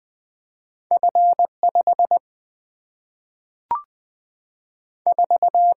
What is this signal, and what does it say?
Morse code, heard as a keyed tone, about 20 wpm, 700 Hz. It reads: F 5 4
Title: Single-Letter Single-Number 20wpm